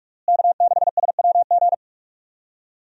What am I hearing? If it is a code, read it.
Morse code, heard as a keyed tone, about 46 wpm, 700 Hz. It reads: K6SWG
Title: USA Call Signs - 5 Characters - Repeated 3x - 45wpm